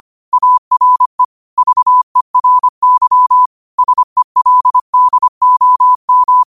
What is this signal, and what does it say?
Morse code, heard as a keyed tone, about 25 wpm, 1000 Hz. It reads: ARE VERY SELDOM
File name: qotdfiles / 25wpm